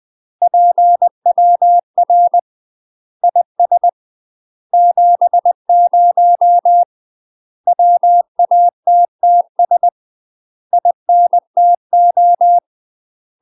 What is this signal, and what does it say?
Morse code, heard as a keyed tone, about 20 wpm, 700 Hz. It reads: PWR IS 70 WATTS INTO
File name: UK MASTER QSOs TRAINING / 20WPM / UK MASTER QSOs 0065